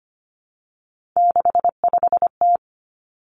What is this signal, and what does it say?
Morse code, heard as a keyed tone, about 25 wpm, 700 Hz. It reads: 65T
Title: Three Challenging Letter-Number Combinations 25wpm